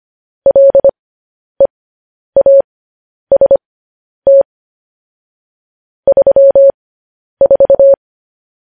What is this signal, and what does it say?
Morse code, heard as a keyed tone, about 25 wpm, spaced as for 10 wpm, 550 Hz. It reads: LEAST 34
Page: news headlines in morse code at 10 wpm